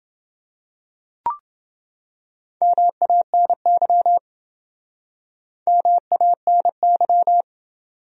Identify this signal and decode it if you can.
Morse code, heard as a keyed tone, about 27 wpm, 700 Hz. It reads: MANY MANY